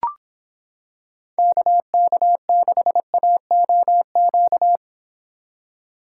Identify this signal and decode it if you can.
Morse code, heard as a keyed tone, about 26 wpm, 700 Hz. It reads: KK6AOQ